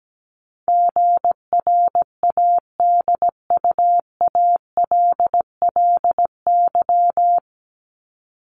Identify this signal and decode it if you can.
Morse code, heard as a keyed tone, about 17 wpm, 700 Hz. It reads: GRADUALLY
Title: LY Suffix - Top 100 Words with Base Words 17wpm